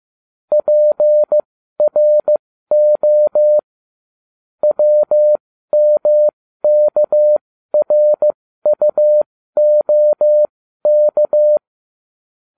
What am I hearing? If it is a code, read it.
Morse code, heard as a keyed tone, about 15 wpm, spaced as for 12 wpm, 600 Hz. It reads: PRO WMKRUOK